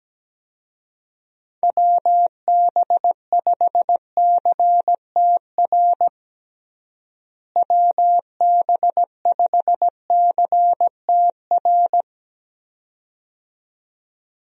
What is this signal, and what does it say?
Morse code, heard as a keyed tone, about 17 wpm, 700 Hz. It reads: WB5CTR WB5CTR
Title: USA Call Signs - 6 Characters - Repeated 2x - 17wpm